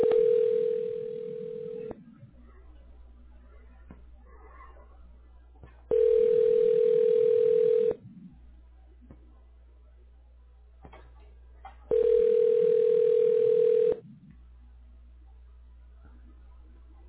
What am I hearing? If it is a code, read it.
Ringback tone (North America), 440 + 480 Hz.